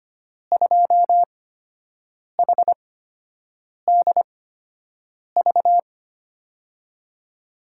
Morse code, heard as a keyed tone, about 25 wpm, spaced as for 7 wpm, 700 Hz. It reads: 2HDV
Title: USA Call Signs - 5 Characters - 8x Character Spacing - 25wpm